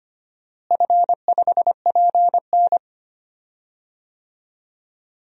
Morse code, heard as a keyed tone, about 25 wpm, 700 Hz. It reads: F5PN
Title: DXCC-100 Call Signs - 4 Characters - Repeated 2x - 25wpm